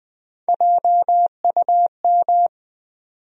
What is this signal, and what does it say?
Morse code, heard as a keyed tone, about 20 wpm, 700 Hz. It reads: JUM
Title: USA Call Sign Postfixes - 3 Letters 20wpm